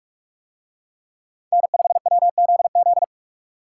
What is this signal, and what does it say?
Morse code, heard as a keyed tone, about 44 wpm, 700 Hz. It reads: N5WZZ